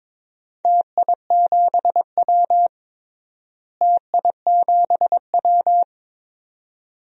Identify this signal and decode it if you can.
Morse code, heard as a keyed tone, about 22 wpm, 700 Hz. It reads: TI7W TI7W
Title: DXCC-100 Call Signs - 4 Characters - Repeated 2x - 22wpm